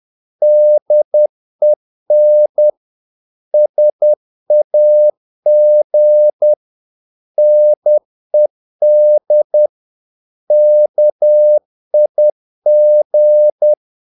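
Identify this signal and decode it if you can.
Morse code, heard as a keyed tone, about 10 wpm, 600 Hz. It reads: DEN SAG NED KIG
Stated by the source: Morse practice of short Danish words at 10 wpm, from CW med Gnister.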